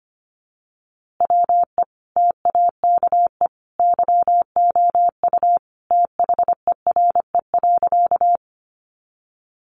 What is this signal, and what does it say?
Morse code, heard as a keyed tone, about 25 wpm, 700 Hz. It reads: WE TAKE YOU THERE.
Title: A Sound Of Thunder 25wpm